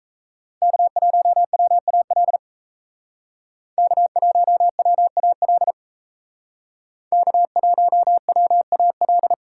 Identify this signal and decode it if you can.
Morse code, heard as a keyed tone, about 38 wpm, 700 Hz. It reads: K1WAL K1WAL K1WAL